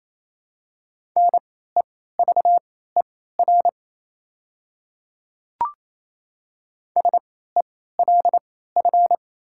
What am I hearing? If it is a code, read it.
Morse code, heard as a keyed tone, about 27 wpm, spaced as for 16 wpm, 700 Hz. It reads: NEVER SELF
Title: 3x Character Spacing - Top 500 Words - 28wpm